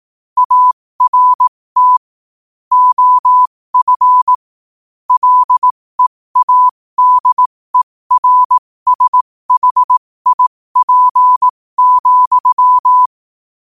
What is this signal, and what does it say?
Morse code, heard as a keyed tone, about 18 wpm, spaced as for 15 wpm, 1000 Hz. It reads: ART OF LEADERSHIP,
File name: qotdfiles / 15wpm